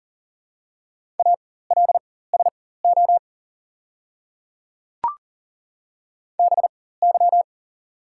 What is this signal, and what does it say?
Morse code, heard as a keyed tone, about 40 wpm, spaced as for 19 wpm, 690 Hz. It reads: ALSO BY
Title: 4x Character Spacing - Top 300 Words - 40wpm